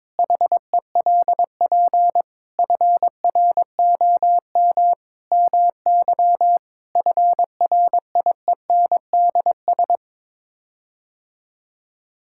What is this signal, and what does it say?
Morse code, heard as a keyed tone, about 22 wpm, 700 Hz. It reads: HELP FROM MY FRIENDS
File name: practice / 22wpm / Sets of 7 Words - 22wpm